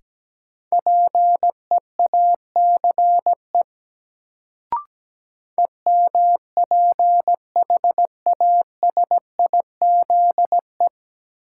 Morse code, heard as a keyed tone, about 17 wpm, 700 Hz. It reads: PEACE EMPHASIZE
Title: Sentences from Top 1500 Words - New Word Drill Alt - 17wpm